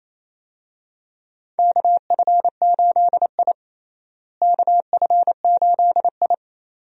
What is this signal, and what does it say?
Morse code, heard as a keyed tone, about 28 wpm, 700 Hz. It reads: KF8I KF8I
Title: USA Call Signs - 4 Characters - Repeated 2x - 28wpm